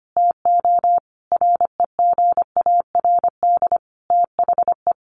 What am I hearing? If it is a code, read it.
Morse code, heard as a keyed tone, about 25 wpm, 700 Hz. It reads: TO REGARD THE
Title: A Sound Of Thunder 25wpm